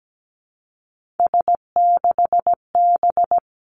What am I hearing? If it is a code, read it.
Morse code, heard as a keyed tone, about 17 wpm, 700 Hz. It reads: S6B
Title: Three Challenging Letter-Number Combinations 17wpm